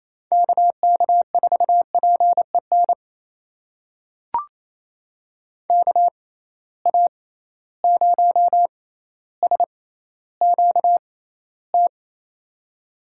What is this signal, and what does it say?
Morse code, heard as a keyed tone, about 28 wpm, spaced as for 14 wpm, 700 Hz. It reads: KK4PEN KA0SQT
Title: USA Call Signs - 6 Characters - 6x Character Spacing - 28wpm